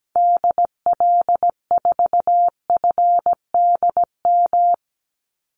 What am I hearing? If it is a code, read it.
Morse code, heard as a keyed tone, about 17 wpm, 700 Hz. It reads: DL4FDM
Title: Call Signs Repeated 2x - 17wpm